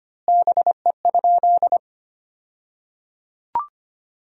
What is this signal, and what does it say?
Morse code, heard as a keyed tone, about 25 wpm, 700 Hz. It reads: BE?
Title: 2x Word-Spacing - Sentences from Top 100 Words - 25wpm